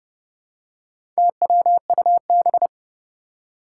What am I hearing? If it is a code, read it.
Morse code, heard as a keyed tone, about 28 wpm, 700 Hz. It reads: TWUB
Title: Four Challenging Letter-Number Combinations 30wpm